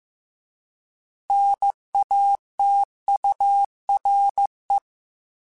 Morse code, heard as a keyed tone, about 15 wpm, 800 Hz. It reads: NATURE
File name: Alphabetical / Alphabetical mp3s / Ns @13